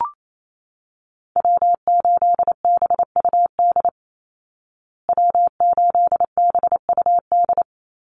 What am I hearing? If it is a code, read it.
Morse code, heard as a keyed tone, about 28 wpm, 700 Hz. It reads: W8BUD W8BUD
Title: Call Signs Repeated 3x - 28wpm